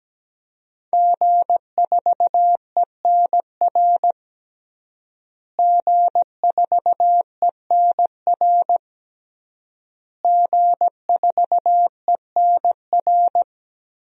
Morse code, heard as a keyed tone, about 17 wpm, 700 Hz. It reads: G4ENR G4ENR G4ENR
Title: DXCC-100 Call Signs - 5 Characters - Repeated 3x - 17wpm